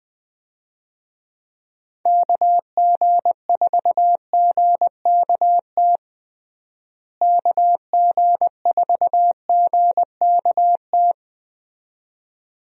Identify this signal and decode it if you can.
Morse code, heard as a keyed tone, about 21 wpm, 700 Hz. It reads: KG4GKT KG4GKT
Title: USA Call Signs - 6 Characters - Repeated 2x - 20wpm